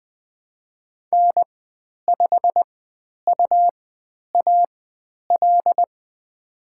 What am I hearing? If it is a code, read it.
Morse code, heard as a keyed tone, about 20 wpm, spaced as for 10 wpm, 700 Hz. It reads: N5UAL
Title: 102 Lesson 17g - Letter u - Call Signs 20wpm